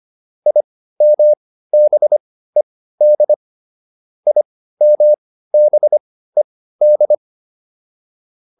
Morse code, heard as a keyed tone, about 25 wpm, spaced as for 15 wpm, 600 Hz. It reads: IMBED IMBED